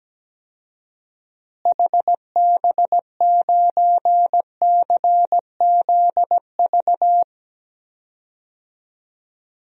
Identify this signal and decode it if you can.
Morse code, heard as a keyed tone, about 17 wpm, 700 Hz. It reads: HB9CZV
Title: DXCC-100 Call Signs - 6 Characters - Repeated 3x - 17wpm